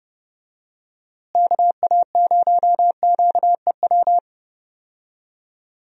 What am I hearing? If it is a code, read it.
Morse code, heard as a keyed tone, about 30 wpm, 700 Hz. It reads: KA0QEW
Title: USA Call Signs - 6 Characters - Repeated 2x - 30wpm